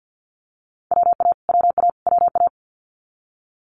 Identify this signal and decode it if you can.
Morse code, heard as a keyed tone, about 50 wpm, 700 Hz. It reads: WA WA WA
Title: Multi-letter Phonemes Repeated 3x 50wpm